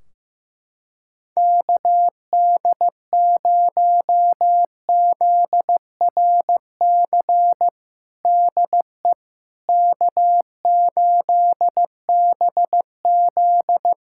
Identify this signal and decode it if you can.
Morse code, heard as a keyed tone, about 15 wpm, 700 Hz. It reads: KD0ZRC DE K8BZ
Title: Practice QSO - Exchange 4 - Ending - 15wpm